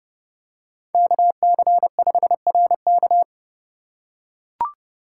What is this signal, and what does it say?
Morse code, heard as a keyed tone, about 30 wpm, 700 Hz. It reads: KC5RK
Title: USA Call Signs - 5 Characters - 2x Character Spacing - 30wpm